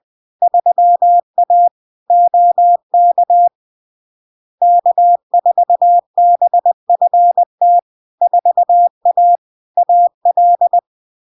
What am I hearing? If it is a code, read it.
Morse code, heard as a keyed tone, about 20 wpm, 700 Hz. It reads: 3A OK K4BFT 4A AL
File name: LARK-FD-2022-Log-in-CW-20wpm-700-Hz